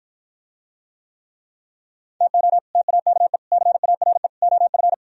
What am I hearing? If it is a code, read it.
Morse code, heard as a keyed tone, about 52 wpm, 700 Hz. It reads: TO TAKE CARE OF